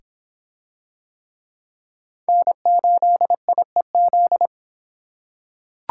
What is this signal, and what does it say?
Morse code, heard as a keyed tone, about 25 wpm, 700 Hz. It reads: N8IEZ